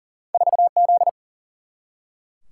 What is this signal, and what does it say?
Morse code, heard as a keyed tone, about 38 wpm, 700 Hz. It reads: 4Z
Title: Single-Letter Single-Number 40wpm